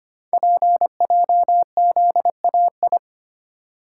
Morse code, heard as a keyed tone, about 25 wpm, 700 Hz. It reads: PJZAI
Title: Code Groups - 5 Letters - 25wpm